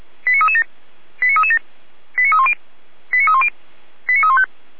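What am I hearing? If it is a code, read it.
Five-tone selective call (ZVEI), heard as quick runs of five tones, tones of about 70 ms. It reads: 89337 89337 89310 89310 89315
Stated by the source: SDR-received ZVEI five-tone call 89304.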